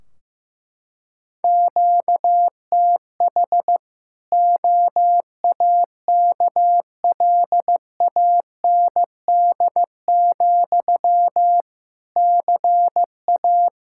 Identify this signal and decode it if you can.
Morse code, heard as a keyed tone, about 15 wpm, 700 Hz. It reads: QTH OAKLAND, CA
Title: Practice QSO - Exchange 1 - Beginning - 15wpm